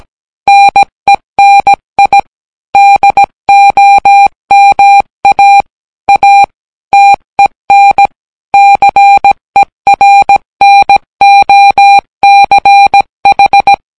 Morse code, heard as a keyed tone, about 18 wpm, 800 Hz. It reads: NENI DOMA A TEN CERNOCH